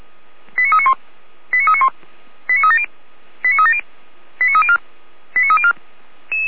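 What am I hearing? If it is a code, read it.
Five-tone selective call (ZVEI), heard as quick runs of five tones, tones of about 70 ms. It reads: 89391 89391 89370 89370 89304 89304
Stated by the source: ZVEI five-tone selective call received on an SDR.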